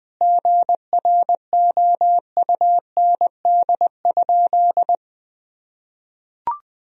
Morse code, heard as a keyed tone, about 20 wpm, 700 Hz. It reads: GROUND?
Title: Sentences from Top 1000 Words - Review Part 3 - 20wpm